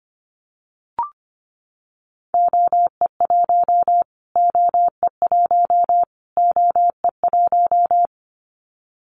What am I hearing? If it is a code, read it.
Morse code, heard as a keyed tone, about 25 wpm, 700 Hz. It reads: OE1 OE1 OE1
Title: Callsign Prefixes for Top 25 DXCC - Repeated 3x - 25wpm